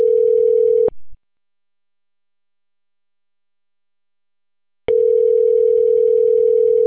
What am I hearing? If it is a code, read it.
Ringback tone (North America), 440 + 480 Hz.